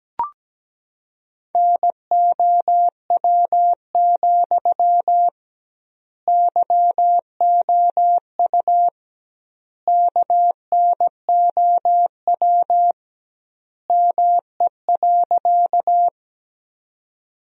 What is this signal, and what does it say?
Morse code, heard as a keyed tone, about 17 wpm, 700 Hz. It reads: NOW, YOU KNOW ME.
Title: 2x Word-Spacing - Sentences from Top 200 Words - 17wpm